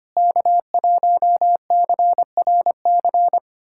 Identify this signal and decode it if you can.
Morse code, heard as a keyed tone, about 25 wpm, 700 Hz. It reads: K1CRC
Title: USA Callsign DE Callsign - 5 Characters - 25wpm